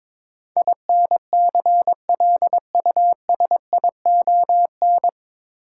Morse code, heard as a keyed tone, about 22 wpm, 700 Hz. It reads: INCLUSION
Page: Morse Code Ninja practice 22wpm